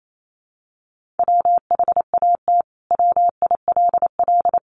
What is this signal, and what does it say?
Morse code, heard as a keyed tone, about 28 wpm, 700 Hz. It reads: WHAT WILL
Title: Sentences from Top 1100 Words 28wpm